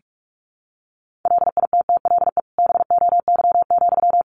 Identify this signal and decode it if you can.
Morse code, heard as a keyed tone, about 44 wpm, 700 Hz. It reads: LITTLE BOY,
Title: Sentences from Top 1000 Words 45wpm